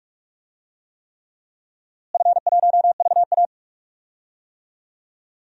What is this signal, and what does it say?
Morse code, heard as a keyed tone, about 43 wpm, 700 Hz. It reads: F1VA
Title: DXCC-100 Call Signs - 4 Characters - Repeated 3x - 45wpm